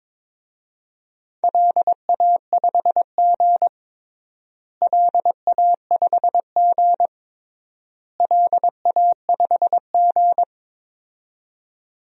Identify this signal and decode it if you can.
Morse code, heard as a keyed tone, about 22 wpm, 700 Hz. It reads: LA5G LA5G LA5G
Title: DXCC-100 Call Signs - 4 Characters - Repeated 3x - 22wpm